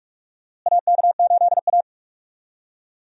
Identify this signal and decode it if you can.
Morse code, heard as a keyed tone, about 45 wpm, 700 Hz. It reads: AK8A
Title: USA Call Signs - 4 Characters - Repeated 3x - 45wpm